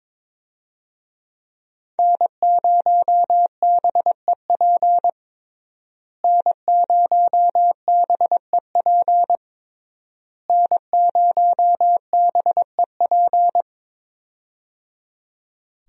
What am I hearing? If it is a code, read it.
Morse code, heard as a keyed tone, about 22 wpm, 700 Hz. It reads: N0BEP N0BEP N0BEP
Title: USA Call Signs - 5 Characters - Repeated 3x - 22wpm